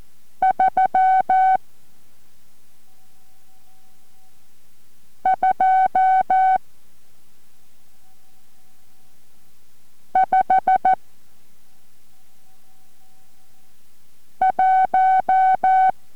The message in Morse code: 3 2 5 1